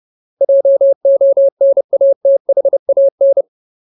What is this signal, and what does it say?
Morse code, heard as a keyed tone, about 30 wpm, 550 Hz. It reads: JONATHAN
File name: Ultimate Word List / 30WPM Ultimate